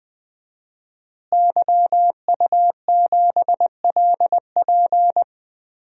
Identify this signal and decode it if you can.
Morse code, heard as a keyed tone, about 20 wpm, 700 Hz. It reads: YU7LP
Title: DXCC-100 Call Signs - 5 Characters - Repeated 2x - 20wpm